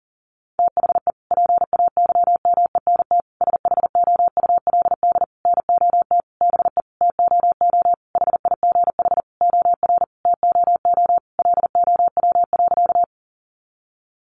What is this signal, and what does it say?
Morse code, heard as a keyed tone, about 37 wpm, 700 Hz. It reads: THE PAYMENT SHOULD NOT BE TOO HIGH OR TOO LOW.
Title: Sentences from Top 1200 Words 40wpm